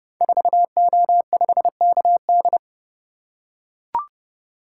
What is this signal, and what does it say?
Morse code, heard as a keyed tone, about 30 wpm, 700 Hz. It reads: 4O5KD
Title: Code Groups - 5 Letter:Numbers - 30wpm